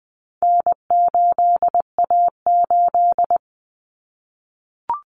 Morse code, heard as a keyed tone, about 20 wpm, 700 Hz. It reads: N8A8
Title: Common Cut Numbers - 4-Digit 20wpm